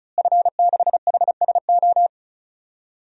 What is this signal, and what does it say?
Morse code, heard as a keyed tone, about 34 wpm, 700 Hz. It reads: F6HSO
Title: DXCC-100 Call Signs - 5 Characters - Repeated 3x - 35wpm